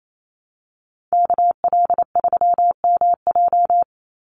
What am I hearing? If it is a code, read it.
Morse code, heard as a keyed tone, about 28 wpm, 700 Hz. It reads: KL3MJ
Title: USA Call Signs 28wpm